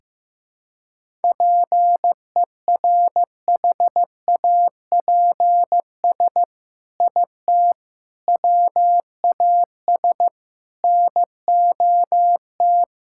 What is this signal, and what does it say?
Morse code, heard as a keyed tone, about 16 wpm, 700 Hz. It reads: PERHAPS IT WAS NOT
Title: Sentences from Top 1000 Words - Review Part 5 - 15wpm